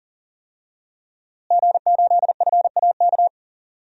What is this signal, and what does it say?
Morse code, heard as a keyed tone, about 37 wpm, 700 Hz. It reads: G8FAK
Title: DXCC-100 Call Signs - 5 Characters - Repeated 2x - 40wpm